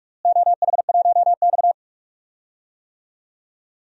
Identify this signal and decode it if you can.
Morse code, heard as a keyed tone, about 41 wpm, 700 Hz. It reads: OH1X
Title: DXCC-100 Call Signs - 4 Characters - Repeated 3x - 45wpm